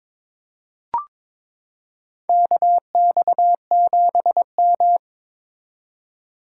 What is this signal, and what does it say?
Morse code, heard as a keyed tone, about 21 wpm, 700 Hz. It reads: KX7M
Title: K1USN SST QSO Practice - 22wpm